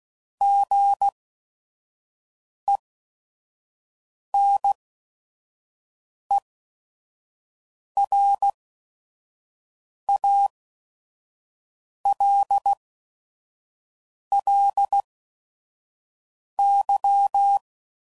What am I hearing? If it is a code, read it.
Morse code, heard as a keyed tone, about 16 wpm, spaced as for 5 wpm, 800 Hz. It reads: GENERALLY
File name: CW Code Groups @5WPM / Weather Sets / Weather 4 14